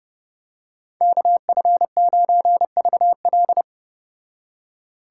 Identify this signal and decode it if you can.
Morse code, heard as a keyed tone, about 30 wpm, 700 Hz. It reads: KF9VL